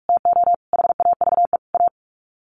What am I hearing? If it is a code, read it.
Morse code, heard as a keyed tone, about 44 wpm, 700 Hz. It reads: TO HAVE A
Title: Sets of 4 Words 45wpm